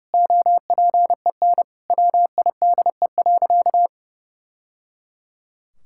Morse code, heard as a keyed tone, about 30 wpm, 700 Hz. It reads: OPEN WIDE.
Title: Sentences from Top 1000 Words - Review Part 6 - 30wpm